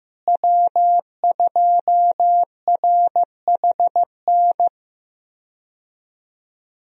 Morse code, heard as a keyed tone, about 15 wpm, 700 Hz. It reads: W2RHN
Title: USA Call Signs - 5 Characters - Repeated 2x - 15wpm